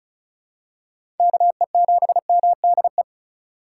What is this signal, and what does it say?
Morse code, heard as a keyed tone, about 34 wpm, 700 Hz. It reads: KE7MDE
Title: USA Call Signs - 6 Characters - 2x Character Spacing - 35wpm